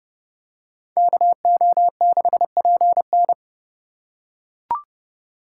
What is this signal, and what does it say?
Morse code, heard as a keyed tone, about 29 wpm, 700 Hz. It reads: KO6PN